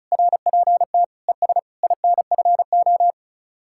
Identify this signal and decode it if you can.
Morse code, heard as a keyed tone, about 43 wpm, 700 Hz. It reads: RPT ES INFO
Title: Speed-Racing Common QSO Phrases - 35wpm